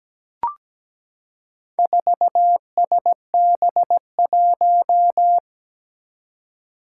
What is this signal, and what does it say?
Morse code, heard as a keyed tone, about 17 wpm, 700 Hz. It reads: 4SB1
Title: Four Challenging Letter-Number Combinations 17wpm